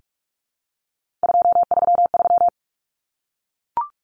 Morse code, heard as a keyed tone, about 44 wpm, 700 Hz. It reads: 233
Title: Numbers - 3-Digit 45wpm